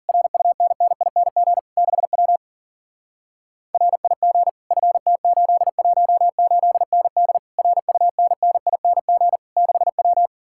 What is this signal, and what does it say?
Morse code, heard as a keyed tone, about 49 wpm, 700 Hz. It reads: RUNNING 6W RIG FT818ND RUNNING 6W
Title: Speed-Racing Common QSO Phrases - 40wpm